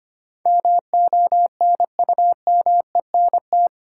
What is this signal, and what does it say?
Morse code, heard as a keyed tone, about 25 wpm, 700 Hz. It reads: MONUMENT
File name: Eight Letter Words - 25wpm